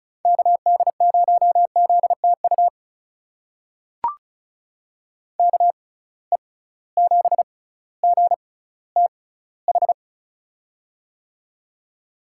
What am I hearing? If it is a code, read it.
Morse code, heard as a keyed tone, about 36 wpm, spaced as for 18 wpm, 700 Hz. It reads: KD0ZTU KE7GTH
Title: USA Call Signs - 6 Characters - 6x Character Spacing - 35wpm